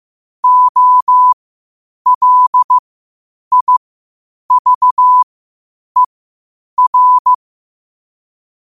Morse code, heard as a keyed tone, about 15 wpm, spaced as for 8 wpm, 1000 Hz. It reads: OLIVER
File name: qotdfiles / 8wpm